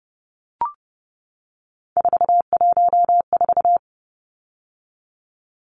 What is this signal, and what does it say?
Morse code, heard as a keyed tone, about 30 wpm, 700 Hz. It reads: 414
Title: Numbers - 3-Digit 30wpm